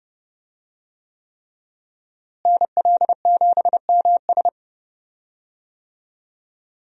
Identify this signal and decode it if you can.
Morse code, heard as a keyed tone, about 29 wpm, 700 Hz. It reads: NL7MS
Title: USA Call Signs - 5 Characters - Repeated 3x - 30wpm